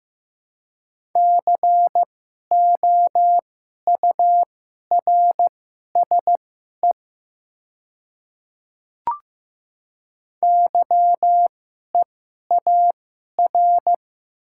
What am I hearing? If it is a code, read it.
Morse code, heard as a keyed tone, about 15 wpm, spaced as for 11 wpm, 700 Hz. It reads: COURSE YEAR
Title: 2x Character Spacing - Top 500 Words - 15wpm